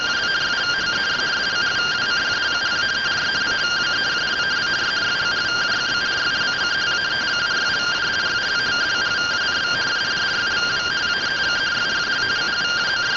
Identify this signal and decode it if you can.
Two-tone teleprinter signal, about 170 Hz shift.